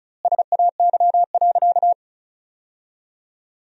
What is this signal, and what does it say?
Morse code, heard as a keyed tone, about 34 wpm, 700 Hz. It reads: SAY.
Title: 4x Word-Spacing - Sentences from Top 500 Words - 35wpm